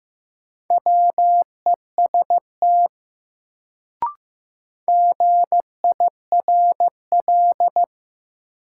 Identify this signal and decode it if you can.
Morse code, heard as a keyed tone, about 15 wpm, 700 Hz. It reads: WEST GIRL
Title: Sentences from Top 400 Words - New Word Drill - 15wpm